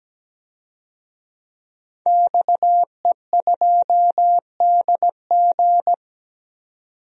Morse code, heard as a keyed tone, about 17 wpm, 700 Hz. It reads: XE2DG
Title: DXCC-100 Call Signs - 5 Characters - 17wpm